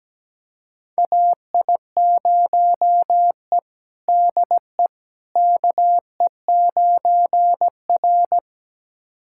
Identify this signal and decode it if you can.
Morse code, heard as a keyed tone, about 17 wpm, 700 Hz. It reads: AI0E DE KE9R
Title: USA Callsign DE Callsign - 4 Characters - 17wpm